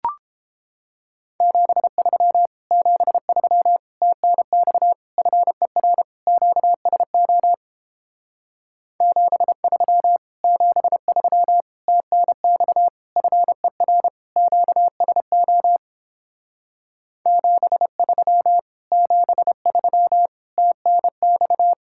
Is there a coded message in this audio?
Yes: Morse code from 1.4 to 21.7 s, heard as a keyed tone, about 29 wpm, 700 Hz. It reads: 73 73 TNX FER QSO 73 73 TNX FER QSO 73 73 TNX